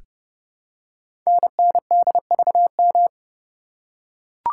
Morse code, heard as a keyed tone, about 30 wpm, 700 Hz. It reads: NNDVM
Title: Code Groups - 5 Letters - 30wpm